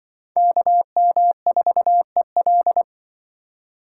Morse code, heard as a keyed tone, about 24 wpm, 700 Hz. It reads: KM4EL
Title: Speed Racing - USA 2x2 Call Signs 20wpm